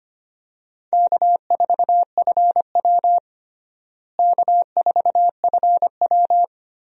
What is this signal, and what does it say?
Morse code, heard as a keyed tone, about 25 wpm, 700 Hz. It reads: K4FW K4FW
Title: USA Call Signs - 4 Characters - Repeated 2x - 25wpm